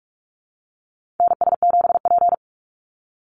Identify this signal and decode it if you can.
Morse code, heard as a keyed tone, about 44 wpm, 700 Hz. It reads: NS7P
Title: NA SOTA Chaser Callsigns 45wpm